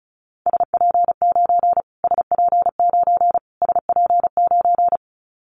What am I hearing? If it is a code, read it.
Morse code, heard as a keyed tone, about 34 wpm, 700 Hz. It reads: SP9 SP9 SP9
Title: Callsign Prefixes for Top 25 DXCC - Repeated 3x - 35wpm